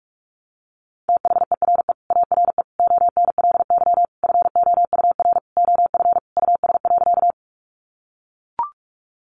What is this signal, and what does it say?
Morse code, heard as a keyed tone, about 44 wpm, 700 Hz. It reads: THERE ARE ONLY FOUR OF US.
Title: Sentences from Top 300 Words 45wpm